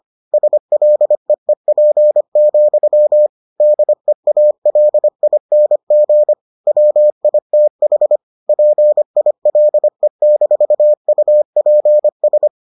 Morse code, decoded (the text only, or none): SLEEP, DEALING WITH PILE-UPS